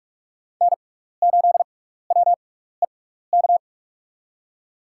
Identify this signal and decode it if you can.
Morse code, heard as a keyed tone, about 42 wpm, spaced as for 15 wpm, 700 Hz. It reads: N8WEK
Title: USA Call Signs - 5 Characters - 6x Character Spacing - 45wpm